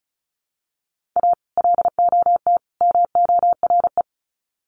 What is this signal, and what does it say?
Morse code, heard as a keyed tone, about 34 wpm, 700 Hz. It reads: A LOT MORE
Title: Sets of 3 Words 35wpm